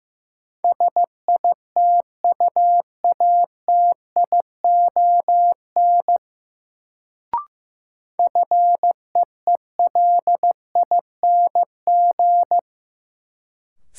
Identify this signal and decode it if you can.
Morse code, heard as a keyed tone, about 15 wpm, 700 Hz. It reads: SITUATION FEELING
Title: Sentences from Top 1200 Words - New Word Drill Alt - 15wpm